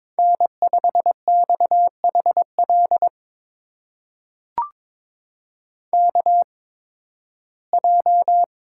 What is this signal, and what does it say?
Morse code, heard as a keyed tone, about 22 wpm, spaced as for 14 wpm, 700 Hz. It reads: N5XHL KJ
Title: USA Call Signs - 8x Character Spacing - 22wpm